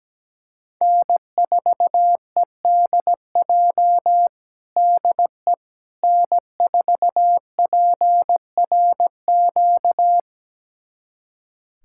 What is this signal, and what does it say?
Morse code, heard as a keyed tone, about 17 wpm, 700 Hz. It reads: N4EDJ DE N4PRQ
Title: USA Callsign DE Callsign - 5 Characters - 17wpm